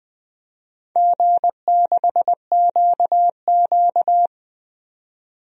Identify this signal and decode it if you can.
Morse code, heard as a keyed tone, about 20 wpm, 700 Hz. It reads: G6QQ
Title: DXCC-100 Call Signs - 4 Characters - 20wpm